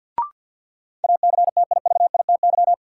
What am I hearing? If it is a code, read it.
Morse code, heard as a keyed tone, about 53 wpm, 700 Hz. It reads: ACTIVITY
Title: Sentences from Top 1000 Words - Review Part 4 - New Word Drill - 50wpm